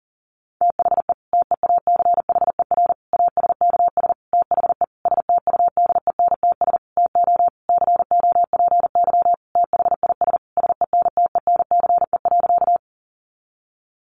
Morse code, heard as a keyed tone, about 40 wpm, 700 Hz. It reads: THE TEACHER ASKS THE STUDENTS TO COPY THIS SENTENCE.
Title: Additional Easy Phrases 40wpm